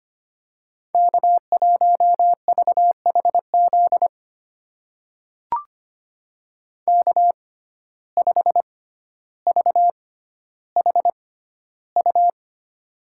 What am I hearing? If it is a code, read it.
Morse code, heard as a keyed tone, about 25 wpm, spaced as for 13 wpm, 700 Hz. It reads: K1VHZ K5VHU
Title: USA Call Signs - 5 Characters - 6x Character Spacing - 25wpm